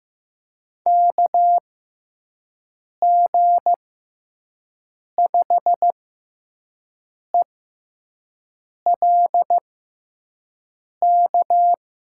Morse code, heard as a keyed tone, about 15 wpm, spaced as for 5 wpm, 700 Hz. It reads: KG5ELK
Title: USA Call Signs - 6 Characters - 6x Character Spacing - 15wpm